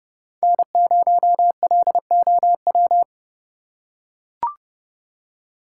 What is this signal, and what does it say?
Morse code, heard as a keyed tone, about 30 wpm, 700 Hz. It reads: N0LOW
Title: USA Call Signs - 5 Characters - 3x Character Spacing - 30wpm